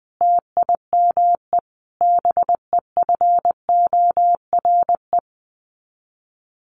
Morse code, heard as a keyed tone, about 20 wpm, 700 Hz. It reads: TIME BEFORE
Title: Sets of 5 Words Encore 20wpm